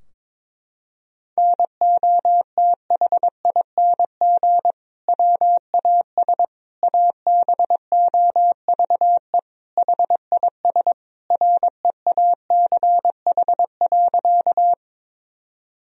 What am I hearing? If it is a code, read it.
Morse code, heard as a keyed tone, about 22 wpm, 700 Hz. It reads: NOTHING WAS ABOVE HIS REACH.
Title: Sentences from Top 1000 Words - Review Part 2 - 22wpm